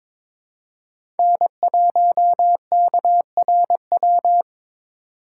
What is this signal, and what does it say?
Morse code, heard as a keyed tone, about 22 wpm, 700 Hz. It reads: N1KRW
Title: USA Call Signs - 3x Character Spacing - 22wpm